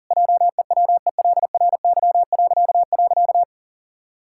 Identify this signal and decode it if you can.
Morse code, heard as a keyed tone, about 40 wpm, 700 Hz. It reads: JEWELRY..
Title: Pangram Challenge v2 - 40wpm